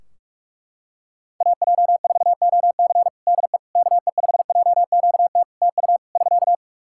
Morse code, heard as a keyed tone, about 44 wpm, 700 Hz. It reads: AJ4OC DE KE5JQT TU <SK>
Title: K1USN SST QSO Practice 45wpm